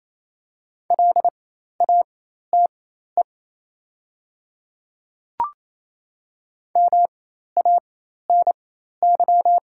Morse code, heard as a keyed tone, about 28 wpm, spaced as for 13 wpm, 700 Hz. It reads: LATE MANY
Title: 4x Character Spacing - Top 300 Words - 28wpm